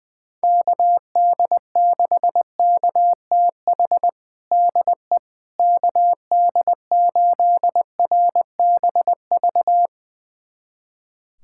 Morse code, heard as a keyed tone, about 21 wpm, 700 Hz. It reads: KD6KTH DE KD8RBV
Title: USA Callsign DE Callsign - 6 Characters - 20wpm